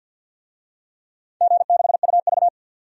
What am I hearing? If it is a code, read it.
Morse code, heard as a keyed tone, about 51 wpm, 700 Hz. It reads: G6UV
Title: DXCC-100 Call Signs - 4 Characters - 50wpm